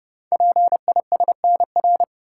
Morse code, heard as a keyed tone, about 29 wpm, 700 Hz. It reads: PISNR
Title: Code Groups - 5 Letters - 30wpm